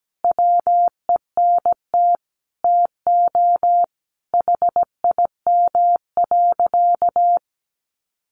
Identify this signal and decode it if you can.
Morse code, heard as a keyed tone, about 17 wpm, 700 Hz. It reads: WENT TO HIM.
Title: Sentences from Top 400 Words 17wpm